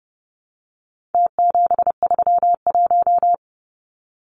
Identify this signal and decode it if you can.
Morse code, heard as a keyed tone, about 30 wpm, 700 Hz. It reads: T731
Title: Numbers - Contest Style - 4-Digit 30wpm